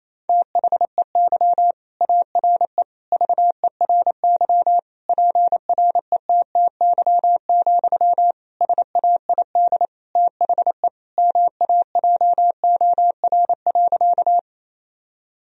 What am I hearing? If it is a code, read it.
Morse code, heard as a keyed tone, about 27 wpm, 700 Hz. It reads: THEY ARE VERY PRETTY, SAID THE MAJOR.
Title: Sentences from Top 1000 Words - Review Part 4 - 28wpm